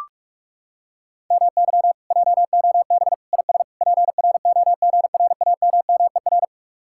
Morse code, heard as a keyed tone, about 37 wpm, 700 Hz. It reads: MY JOB IS PROGRAMMER
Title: Speed-Racing Common QSO Phrases Encore - 30wpm